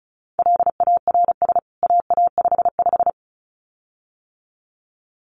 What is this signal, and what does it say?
Morse code, heard as a keyed tone, about 34 wpm, 700 Hz. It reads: LARS AA55
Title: CWT Exchange 35wpm